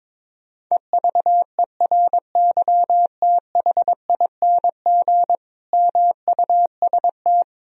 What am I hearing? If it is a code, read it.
Morse code, heard as a keyed tone, about 22 wpm, 700 Hz. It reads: EVERYTHING MUST
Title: Sentences from Top 1000 Words - Review Part 4 - 22wpm